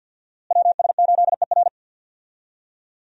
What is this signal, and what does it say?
Morse code, heard as a keyed tone, about 47 wpm, 700 Hz. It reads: WS8ER
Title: USA Call Signs - 5 Characters - Repeated 3x - 50wpm